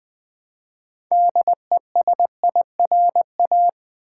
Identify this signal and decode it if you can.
Morse code, heard as a keyed tone, about 20 wpm, 700 Hz. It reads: DESIRA
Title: Build A Word - 3 Letters At A Time - 9-Letter Words - 20wpm